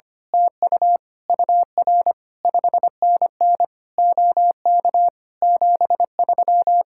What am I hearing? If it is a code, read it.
Morse code, heard as a keyed tone, about 24 wpm, 700 Hz. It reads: TU UR 5NN OK 73
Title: Speed-Racing Common QSO Phrases - 25wpm